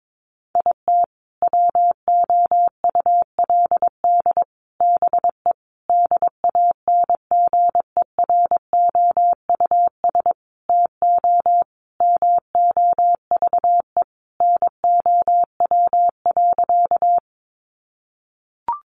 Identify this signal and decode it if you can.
Morse code, heard as a keyed tone, about 22 wpm, 700 Hz. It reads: IT WOULD BE DANGEROUS TO MOVE NOW.
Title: Sentences from Top 1200 Words 22wpm